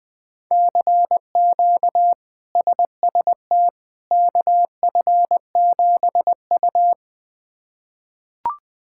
Morse code, heard as a keyed tone, about 21 wpm, 700 Hz. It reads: CQ SST KF7U
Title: K1USN SST QSO Practice - 20wpm